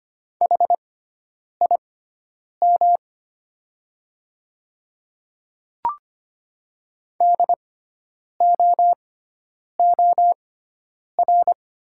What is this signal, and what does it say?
Morse code, heard as a keyed tone, about 25 wpm, spaced as for 9 wpm, 690 Hz. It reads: HIM DOOR
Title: 6x Character Spacing - Top 300 Words - 25wpm